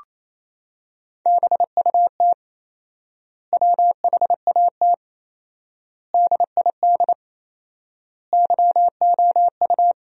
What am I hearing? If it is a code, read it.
Morse code, heard as a keyed tone, about 28 wpm, 700 Hz. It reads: BUT WHAT DID YOU
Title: 4x Word-Spacing - Sentences from Top 300 Words - 28wpm